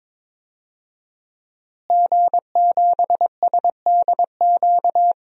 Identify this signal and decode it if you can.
Morse code, heard as a keyed tone, about 21 wpm, 700 Hz. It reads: G7SDQ
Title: DXCC-100 Call Signs - 5 Characters - Repeated 3x - 22wpm